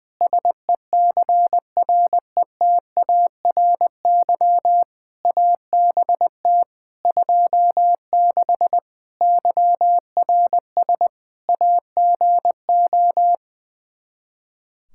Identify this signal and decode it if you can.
Morse code, heard as a keyed tone, about 20 wpm, 700 Hz. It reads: SECRETARY ABT 26 YRS AGO
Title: K1USN SST QSO Practice 20wpm